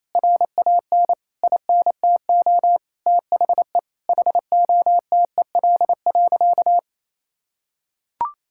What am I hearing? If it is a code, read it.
Morse code, heard as a keyed tone, about 28 wpm, 700 Hz. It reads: RAN INTO THE HOTEL.
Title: Sentences from Top 1600 Words 28wpm